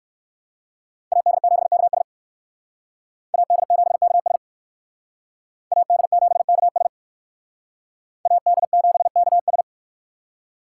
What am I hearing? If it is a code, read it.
Morse code, heard as a keyed tone, about 64 wpm, 700 Hz. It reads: AD7KS AD7KS AD7KS AD7KS